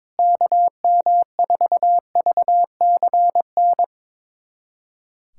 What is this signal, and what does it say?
Morse code, heard as a keyed tone, about 22 wpm, 700 Hz. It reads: KM4VCN
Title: USA Callsign DE Callsign - 6 Characters - 22wpm